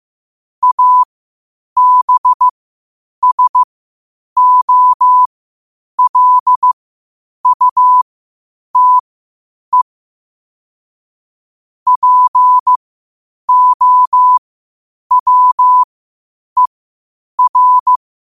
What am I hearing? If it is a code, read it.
Morse code, heard as a keyed tone, about 15 wpm, spaced as for 8 wpm, 1000 Hz. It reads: ABSOLUTE POWER
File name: qotdfiles / 8wpm